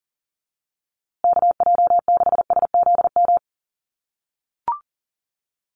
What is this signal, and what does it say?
Morse code, heard as a keyed tone, about 40 wpm, 700 Hz. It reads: KJ6SZM
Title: USA Call Signs Repeated 2x 40wpm